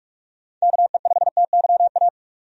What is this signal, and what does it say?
Morse code, heard as a keyed tone, about 46 wpm, 700 Hz. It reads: KE5TYA
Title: USA Call Signs - 6 Characters - 45wpm